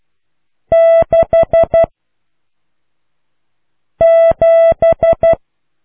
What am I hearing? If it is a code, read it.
Morse code, heard as a keyed tone, about 12 wpm, 660 Hz. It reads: 6 7